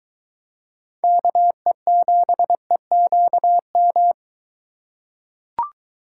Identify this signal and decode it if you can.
Morse code, heard as a keyed tone, about 23 wpm, 700 Hz. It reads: KE7EQM